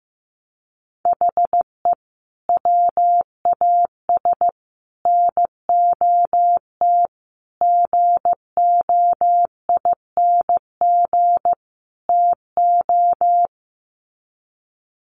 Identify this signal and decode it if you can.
Morse code, heard as a keyed tone, about 15 wpm, 700 Hz. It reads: HE WAS NOT GOING TO
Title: Sets of 5 Words 15wpm